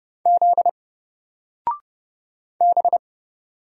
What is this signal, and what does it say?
Morse code, heard as a keyed tone, about 30 wpm, 700 Hz. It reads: Z B
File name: Single-Letter-Number-30wpm